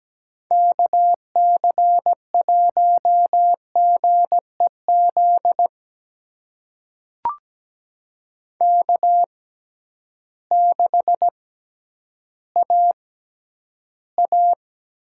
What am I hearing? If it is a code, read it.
Morse code, heard as a keyed tone, about 17 wpm, spaced as for 10 wpm, 700 Hz. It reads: KC1GEZ K6AA
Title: USA Call Signs - 6x Character Spacing - 17wpm